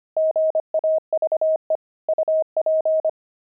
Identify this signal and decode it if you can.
Morse code, heard as a keyed tone, about 25 wpm, 630 Hz. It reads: GAVE UP
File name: pangram-fs-25-cs-25-friars